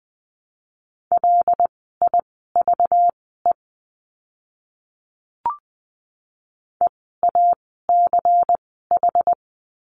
Morse code, heard as a keyed tone, about 20 wpm, spaced as for 15 wpm, 700 Hz. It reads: LIVE EACH
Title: 192 Lesson 31f - Letter v - Words 20wpm